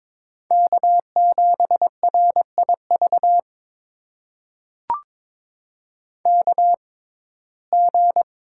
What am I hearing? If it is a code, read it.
Morse code, heard as a keyed tone, about 22 wpm, spaced as for 16 wpm, 700 Hz. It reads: K7RIV KG
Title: USA Call Signs - 6x Character Spacing - 22wpm